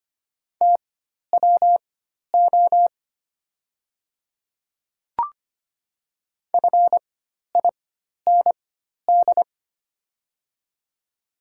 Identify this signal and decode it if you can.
Morse code, heard as a keyed tone, about 25 wpm, spaced as for 12 wpm, 690 Hz. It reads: TWO FIND
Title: 4x Character Spacing - Top 200 Words - 25wpm